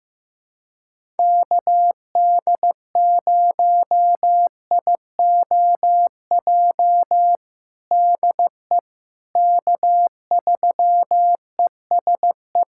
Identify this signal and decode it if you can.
Morse code, heard as a keyed tone, about 15 wpm, 700 Hz. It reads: KD0IOJ DE K3ESE
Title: USA Callsign DE Callsign 15wpm